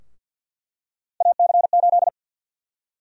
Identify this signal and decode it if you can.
Morse code, heard as a keyed tone, about 46 wpm, 700 Hz. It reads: AC8
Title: USA Call Sign Prefixes - Repeated 2x - 50wpm